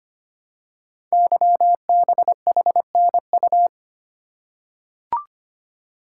Morse code, heard as a keyed tone, about 25 wpm, 700 Hz. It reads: YBHNU
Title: Code Groups - 5 Letter:Numbers - 25wpm